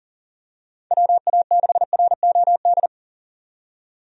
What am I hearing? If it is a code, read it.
Morse code, heard as a keyed tone, about 38 wpm, 700 Hz. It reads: WA6ROD
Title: USA Call Signs - 6 Characters - 40wpm